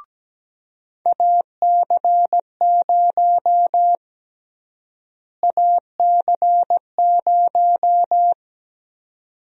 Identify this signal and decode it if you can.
Morse code, heard as a keyed tone, about 17 wpm, 700 Hz. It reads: AC0 AC0
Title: USA Call Sign Prefixes - Repeated 2x - 17wpm